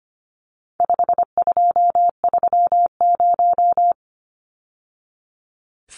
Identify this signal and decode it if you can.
Morse code, heard as a keyed tone, about 25 wpm, 700 Hz. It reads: 5230
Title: Numbers - 4-Digit 25wpm